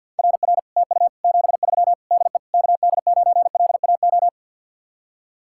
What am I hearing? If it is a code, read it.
Morse code, heard as a keyed tone, about 64 wpm, 700 Hz. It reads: RR TU 73 DE KD9LAO